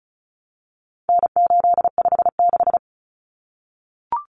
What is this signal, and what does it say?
Morse code, heard as a keyed tone, about 34 wpm, 700 Hz. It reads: N856
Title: Common Cut Numbers - 4-Digit 35wpm